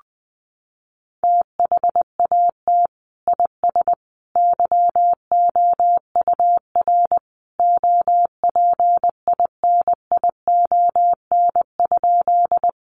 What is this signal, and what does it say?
Morse code, heard as a keyed tone, about 20 wpm, 700 Hz. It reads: THAT IS YOUR OPINION?
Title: Sentences from Top 1200 Words 20wpm